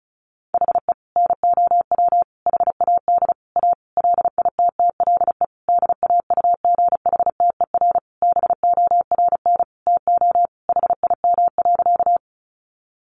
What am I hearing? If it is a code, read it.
Morse code, heard as a keyed tone, about 34 wpm, 700 Hz. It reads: HE NOW HAD A LITTLE DAUGHTER BORN TO HIM.
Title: Sentences from Top 1500 Words 35wpm